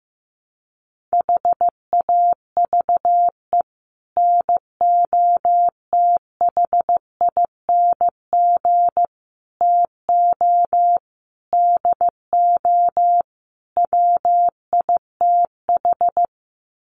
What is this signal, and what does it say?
Morse code, heard as a keyed tone, about 15 wpm, 700 Hz. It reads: HAVE NOTHING TO DO WITH
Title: Sets of 5 Words 15wpm